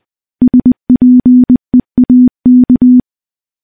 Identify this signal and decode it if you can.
Morse code, heard as a keyed tone, about 21 wpm, 250 Hz. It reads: SPEAK
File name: morse-training-20WPM